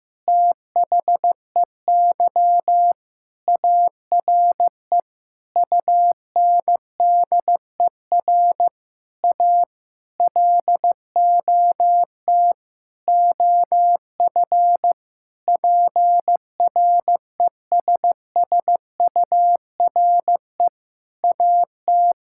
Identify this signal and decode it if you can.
Morse code, heard as a keyed tone, about 15 wpm, 700 Hz. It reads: THEY ARE UNDER A LOT OF PRESSURE AT
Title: Sentences from Top 1000 Words - Review Part 6 - 15wpm